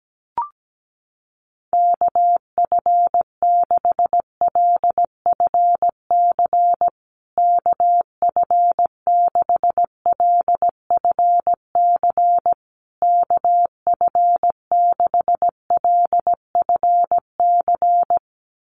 Morse code, heard as a keyed tone, about 17 wpm, 700 Hz. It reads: KF6LFC KF6LFC KF6LFC
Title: USA Call Signs Repeated 3x 17wpm